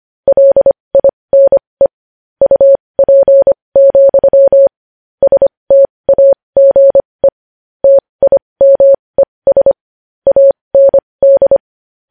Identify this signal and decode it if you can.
Morse code, heard as a keyed tone, about 25 wpm, spaced as for 20 wpm, 550 Hz. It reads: LINE UP, STAGE TIMES AND
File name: podcast / News headlines-1x-25x20